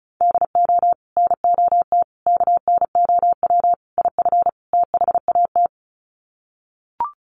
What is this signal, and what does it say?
Morse code, heard as a keyed tone, about 34 wpm, 700 Hz. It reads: DO NOT KNOW IF THAT
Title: Sets of 5 Words Encore2 35wpm